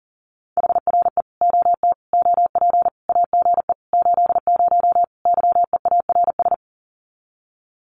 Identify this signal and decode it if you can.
Morse code, heard as a keyed tone, about 40 wpm, 700 Hz. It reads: HRE OT OP AGE 80 YEARS
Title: Common QSO Phrases 40wpm